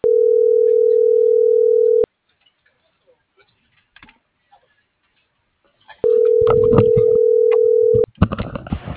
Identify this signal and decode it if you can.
Ringback tone (North America), 440 + 480 Hz.
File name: CENTRAL SOTOVA / out alejandro.rodriguez - .47244